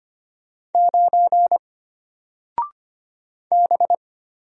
Morse code, heard as a keyed tone, about 25 wpm, 700 Hz. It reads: 9 B
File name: Single-Letter-Number-25wpm